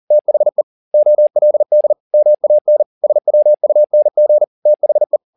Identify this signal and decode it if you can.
Morse code, heard as a keyed tone, about 40 wpm, 600 Hz. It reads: THE OLD MAN SWUNG THE
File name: the old man and the sea 2 40wpm